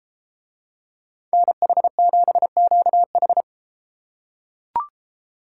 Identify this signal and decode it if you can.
Morse code, heard as a keyed tone, about 31 wpm, 700 Hz. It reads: NH7QH